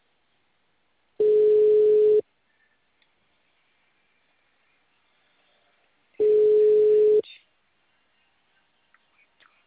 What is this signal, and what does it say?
Ringback tone (Europe, 425 Hz).
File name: out- -2011- - - .5